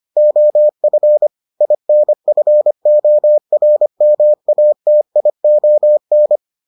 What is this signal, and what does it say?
Morse code, heard as a keyed tone, about 25 wpm, 600 Hz. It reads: OF INFORMATION